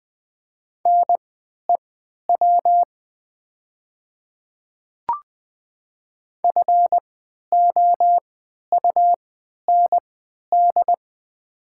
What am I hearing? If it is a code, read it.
Morse code, heard as a keyed tone, about 21 wpm, spaced as for 11 wpm, 700 Hz. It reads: NEW FOUND
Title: 3x Character Spacing - Top 500 Words - 20wpm